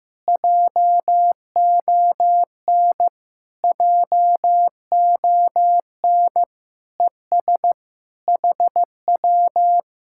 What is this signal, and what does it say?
Morse code, heard as a keyed tone, about 15 wpm, 700 Hz. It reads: JON JON ES HW
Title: Speed-Racing Common QSO Phrases Encore - 15wpm